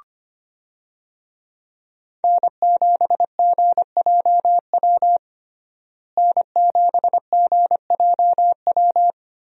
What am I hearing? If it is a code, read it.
Morse code, heard as a keyed tone, about 25 wpm, 700 Hz. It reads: N7GJW N7GJW
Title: USA Call Signs - 5 Characters - Repeated 2x - 25wpm